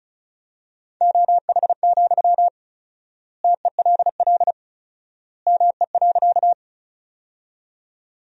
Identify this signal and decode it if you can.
Morse code, heard as a keyed tone, about 36 wpm, 700 Hz. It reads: OH, TELL ME.
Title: 4x Word-Spacing - Sentences from Top 500 Words - 35wpm